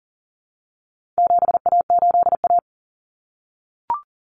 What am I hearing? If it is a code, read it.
Morse code, heard as a keyed tone, about 40 wpm, 700 Hz. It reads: 7A8A
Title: Common Cut Numbers - 4-Digit 40wpm